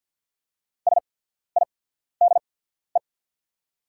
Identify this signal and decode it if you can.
Morse code, heard as a keyed tone, about 48 wpm, spaced as for 14 wpm, 690 Hz. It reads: SIDE
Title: 8x Character Spacing - Top 200 Words - 50wpm